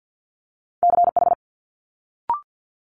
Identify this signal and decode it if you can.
Morse code, heard as a keyed tone, about 50 wpm, 700 Hz. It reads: CH
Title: Multi-letter Phonemes Quick Fire 50wpm